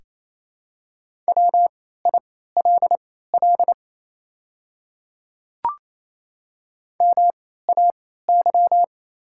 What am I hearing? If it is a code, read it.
Morse code, heard as a keyed tone, about 28 wpm, spaced as for 16 wpm, 690 Hz. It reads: WILL MAY
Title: 3x Character Spacing - Top 100 Words - 28wpm